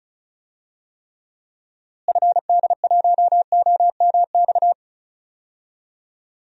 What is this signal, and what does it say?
Morse code, heard as a keyed tone, about 34 wpm, 700 Hz. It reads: FD1OMX
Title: DXCC-100 Call Signs - 6 Characters - Repeated 2x - 35wpm